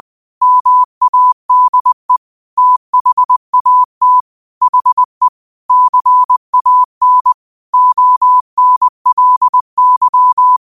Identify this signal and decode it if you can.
Morse code, heard as a keyed tone, about 20 wpm, 1000 Hz. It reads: MADE THAT HE CAN ONLY
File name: qotdfiles / 20wpm